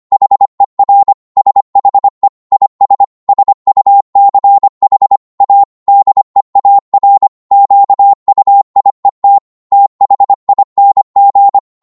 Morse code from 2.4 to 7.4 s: IS SUCH A DEAR